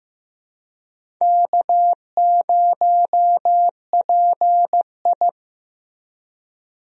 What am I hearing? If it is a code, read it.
Morse code, heard as a keyed tone, about 15 wpm, 700 Hz. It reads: K0PI
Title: USA Call Signs - 4 Characters - Repeated 3x - 15wpm